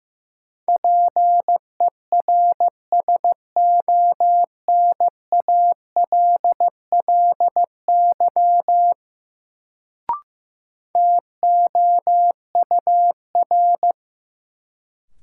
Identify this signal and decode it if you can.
Morse code, heard as a keyed tone, about 15 wpm, 700 Hz. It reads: PERSONALLY TOUR
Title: Sentences from Top 1400 Words - New Word Drill - 15wpm